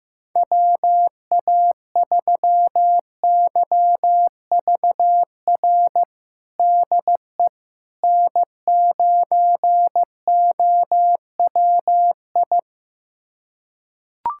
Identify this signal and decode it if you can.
Morse code, heard as a keyed tone, about 15 wpm, 700 Hz. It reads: WA3YVR DE N9OWI
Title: USA Callsign DE Callsign 15wpm